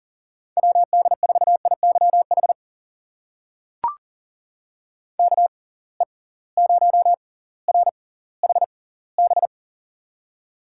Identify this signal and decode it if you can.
Morse code, heard as a keyed tone, about 40 wpm, spaced as for 20 wpm, 700 Hz. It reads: WD4IYH KE0RHB
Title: USA Call Signs - 6 Characters - 6x Character Spacing - 40wpm